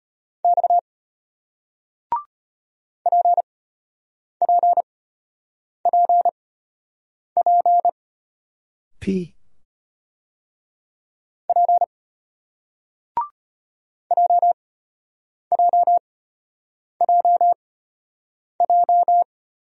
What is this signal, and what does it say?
Morse code, heard as a keyed tone, about 35 wpm, spaced as for 8 wpm, 700 Hz. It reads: X PPPP P JJJJ